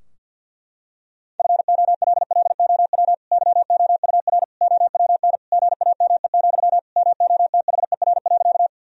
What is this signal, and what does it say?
Morse code, heard as a keyed tone, about 50 wpm, 700 Hz. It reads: FOLLOW YOUR OWN GAME, MOTHER.
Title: Sentences from Top 1000 Words - Review Part 3 - 50wpm